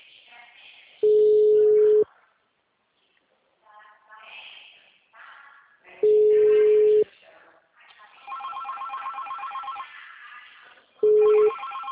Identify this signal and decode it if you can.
Ringback tone (Europe, 425 Hz), with speech.
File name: out- -2038- - - .27273